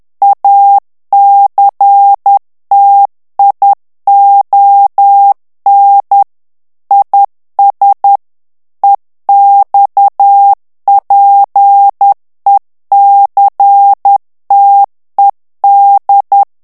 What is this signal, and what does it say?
Morse code, heard as a keyed tone, about 10 wpm, 800 Hz. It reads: ACTION IS EXPECTED
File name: ANGLAIS texte clair 3 vitesse 10 mots minute